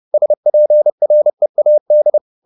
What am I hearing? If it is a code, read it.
Morse code, heard as a keyed tone, about 30 wpm, 600 Hz. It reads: SPREAD